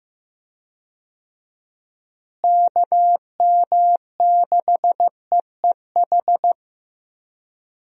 Morse code, heard as a keyed tone, about 15 wpm, 700 Hz. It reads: KM6EEH
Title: USA Call Signs - 6 Characters - Repeated 2x - 15wpm